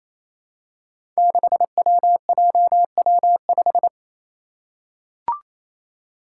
Morse code, heard as a keyed tone, about 28 wpm, 700 Hz. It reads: 6WJW5
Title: Five Challenging Letter-Number Combinations 28wpm